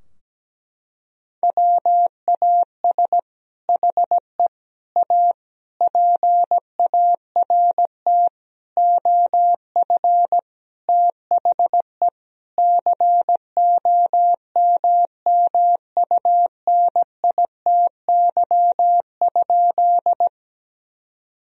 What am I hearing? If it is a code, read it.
Morse code, heard as a keyed tone, about 17 wpm, 700 Hz. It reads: WAS HE A PART OF THE COMMUNITY?
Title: Sentences from Top 1000 Words - Review Part 4 - 17wpm